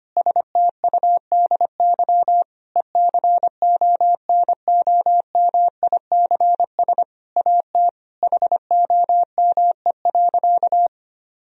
Morse code, heard as a keyed tone, about 25 wpm, 700 Hz. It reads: STUDY ECONOMICS AT HOME.
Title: Sentences from Top 1000 Words - Review Part 3 - 25wpm